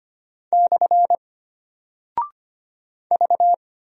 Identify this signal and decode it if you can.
Morse code, heard as a keyed tone, about 25 wpm, 700 Hz. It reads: / V
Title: Single Character (Letter:Number:Punctuation:Prosign) Alt 25wpm